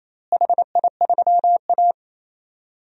Morse code, heard as a keyed tone, about 28 wpm, 700 Hz. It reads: HI3A
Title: DXCC-100 Call Signs - 4 Characters - Repeated 2x - 28wpm